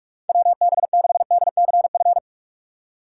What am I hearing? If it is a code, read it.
Morse code, heard as a keyed tone, about 46 wpm, 700 Hz. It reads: WB6DCF
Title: USA Call Signs - 6 Characters - 4x Character Spacing - 45wpm